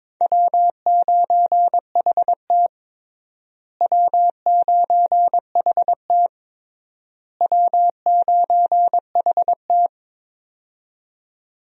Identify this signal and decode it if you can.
Morse code, heard as a keyed tone, about 22 wpm, 700 Hz. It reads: W9HT W9HT W9HT
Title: USA Call Signs - 4 Characters - Repeated 3x - 22wpm